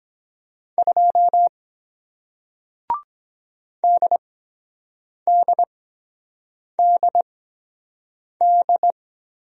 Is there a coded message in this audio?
Yes: Morse code from 0.8 to 8.9 s, heard as a keyed tone, about 23 wpm, spaced as for 7 wpm, 700 Hz. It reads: 2 DDDD